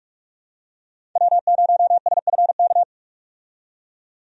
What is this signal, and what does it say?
Morse code, heard as a keyed tone, about 46 wpm, 700 Hz. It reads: W0SFK
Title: USA Call Signs - 5 Characters - 45wpm